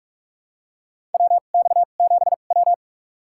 Morse code, heard as a keyed tone, about 47 wpm, spaced as for 33 wpm, 700 Hz. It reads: WX7W
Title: USA Call Signs - 4 Characters - 2x Character Spacing - 45wpm